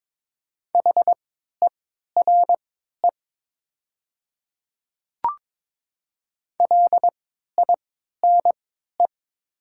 Morse code, heard as a keyed tone, about 22 wpm, spaced as for 12 wpm, 700 Hz. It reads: HERE LINE
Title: 3x Character Spacing - Top 300 Words - 22wpm